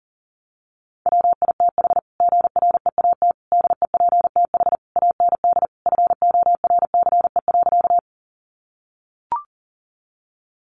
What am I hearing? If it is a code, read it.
Morse code, heard as a keyed tone, about 40 wpm, 700 Hz. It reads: WITH GREAT DEPTH AND FORCE.
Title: Sentences from Top 1200 Words 40wpm